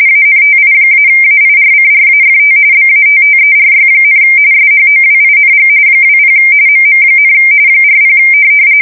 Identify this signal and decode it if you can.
RTTY, about 45.45 baud. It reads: ANTI-AIRCRAFT BATTERY, SHOWING 20 MM, 40
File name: fsk170 / USS-Massachusetts